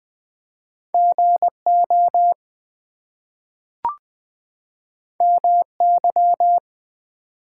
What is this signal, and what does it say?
Morse code, heard as a keyed tone, about 20 wpm, 700 Hz. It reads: GO MY
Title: Two Letter Words - 20wpm